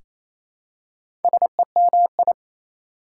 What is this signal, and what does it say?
Morse code, heard as a keyed tone, about 27 wpm, 700 Hz. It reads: SEMI
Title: Prefixes and Suffixes 28wpm